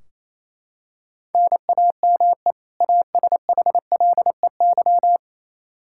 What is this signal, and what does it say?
Morse code, heard as a keyed tone, about 28 wpm, 700 Hz. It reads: NAME ASHLEY
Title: Practice QSO - Exchange 1 - Beginning - 28wpm